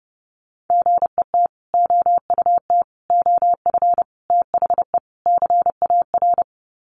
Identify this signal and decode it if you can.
Morse code, heard as a keyed tone, about 30 wpm, 700 Hz. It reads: GET OUT OF THE CAR
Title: Sets of 5 Words Encore2 30wpm